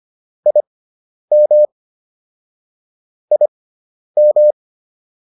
Morse code, heard as a keyed tone, about 25 wpm, spaced as for 10 wpm, 600 Hz. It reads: IM IM